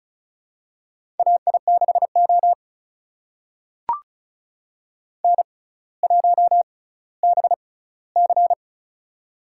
Morse code, heard as a keyed tone, about 36 wpm, spaced as for 18 wpm, 700 Hz. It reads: AI6O N1BC
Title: USA Call Signs - 4 Characters - 6x Character Spacing - 35wpm